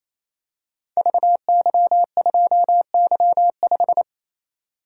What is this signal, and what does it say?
Morse code, heard as a keyed tone, about 29 wpm, 700 Hz. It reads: VY2Y5
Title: Code Groups - 5 Letter:Numbers - 28wpm